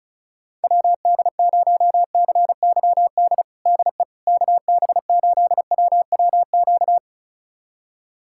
Morse code, heard as a keyed tone, about 34 wpm, 700 Hz. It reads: WD0CYD DE KB8WWQ
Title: USA Callsign DE Callsign - 6 Characters - 35wpm